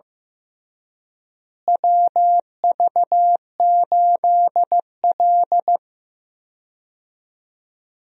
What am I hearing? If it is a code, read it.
Morse code, heard as a keyed tone, about 15 wpm, 700 Hz. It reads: WV8L
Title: USA Call Signs - 4 Characters - Repeated 2x - 15wpm